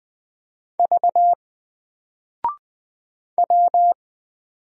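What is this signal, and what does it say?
Morse code, heard as a keyed tone, about 20 wpm, 700 Hz. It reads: V W
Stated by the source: Single letter and number Morse drill at 20 wpm, from N4LCD.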